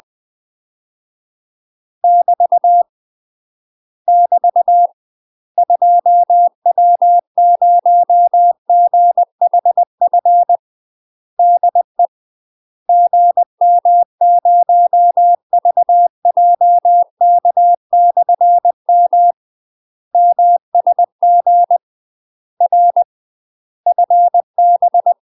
Morse code, heard as a keyed tone, about 20 wpm, 700 Hz. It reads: <BT> <BT> 2W0GHF DE GM0VJK/M MSG R FB
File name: UK MASTER QSOs TRAINING/20WPM